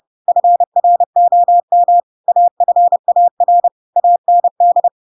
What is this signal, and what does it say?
Morse code, heard as a keyed tone, about 30 wpm, 700 Hz. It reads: FROM AFAR AND
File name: Pinocchio / Pinocchio ch3 30 30 ik1hge